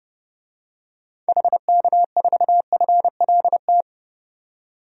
Morse code, heard as a keyed tone, about 31 wpm, 700 Hz. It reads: HK4FLT
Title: DXCC-100 Call Signs - 6 Characters - 30wpm